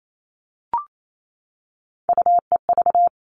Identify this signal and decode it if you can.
Morse code, heard as a keyed tone, about 28 wpm, 700 Hz. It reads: UEV
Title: All Cut Numbers - 3-Digit 28wpm